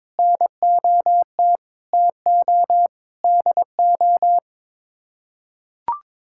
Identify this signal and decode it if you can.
Morse code, heard as a keyed tone, about 22 wpm, 700 Hz. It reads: NOT TO DO
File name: practice / 22wpm / Sets of 8 Words - 22wpm